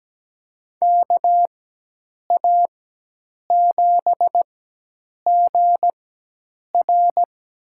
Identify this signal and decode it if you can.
Morse code, heard as a keyed tone, about 17 wpm, spaced as for 8 wpm, 700 Hz. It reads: KA7GR
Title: USA Call Signs - 4x Character Spacing - 17wpm